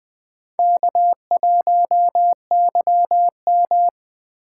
Morse code, heard as a keyed tone, about 20 wpm, 700 Hz. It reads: K1YM
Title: Speed Racing - USA 1x2 Call Signs 17wpm